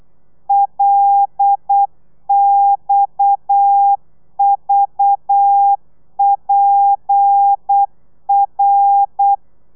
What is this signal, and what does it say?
Morse code, heard as a keyed tone, about 8 wpm, 800 Hz. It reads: LXVPR